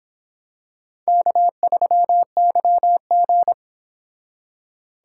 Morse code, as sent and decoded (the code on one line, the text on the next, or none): -.- ...-- -.-- --.
K3YG